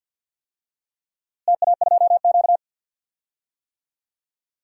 Morse code, heard as a keyed tone, about 51 wpm, 700 Hz. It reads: TA2Q
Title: DXCC-100 Call Signs - 4 Characters - Repeated 2x - 50wpm